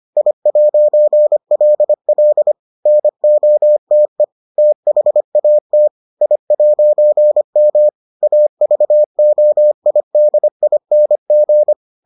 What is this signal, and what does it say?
Morse code, heard as a keyed tone, about 25 wpm, 600 Hz. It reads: I'LL NOTE THAT I'M AVOIDING